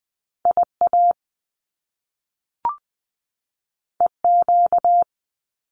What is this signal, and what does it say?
Morse code, heard as a keyed tone, about 20 wpm, 700 Hz. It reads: I A E Q
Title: Top Two Letter Combinations in Words 20wpm